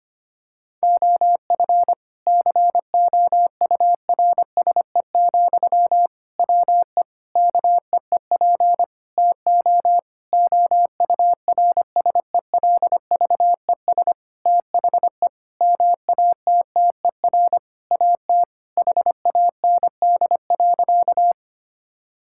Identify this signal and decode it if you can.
Morse code, heard as a keyed tone, about 25 wpm, 700 Hz. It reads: OF COURSE, WE KEEP TO OURSELVES THE MATTER AT HAND.
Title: Sentences from Top 1600 Words 25wpm